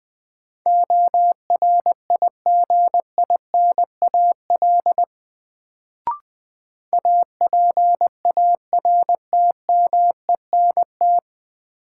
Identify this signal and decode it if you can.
Morse code, heard as a keyed tone, about 21 wpm, 700 Hz. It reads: ORIGINAL APARTMENT
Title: Sentences from Top 900 Words - New Word Drill - 20wpm